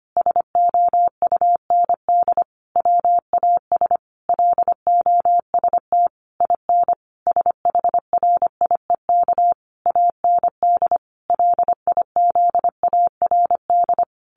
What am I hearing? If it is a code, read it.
Morse code, heard as a keyed tone, about 25 wpm, 700 Hz. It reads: SOUND WAS LOST IN SHRIEK AND LIZARD
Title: A Sound Of Thunder 25wpm